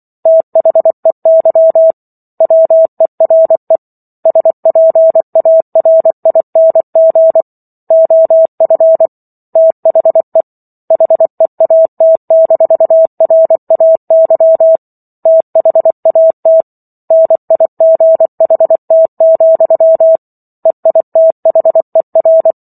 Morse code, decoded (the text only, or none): THEY WERE SPARING OF THE HEAT-RAY THAT NIGHT, EITHER